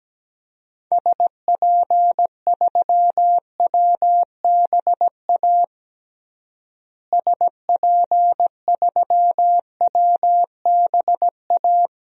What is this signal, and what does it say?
Morse code, heard as a keyed tone, about 17 wpm, 700 Hz. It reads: SP3WBA SP3WBA
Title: DXCC-100 Call Signs - 6 Characters - Repeated 2x - 17wpm